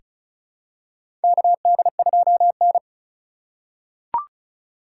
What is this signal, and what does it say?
Morse code, heard as a keyed tone, about 36 wpm, 700 Hz. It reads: KD2N
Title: USA Call Signs - 4x Character Spacing - 35wpm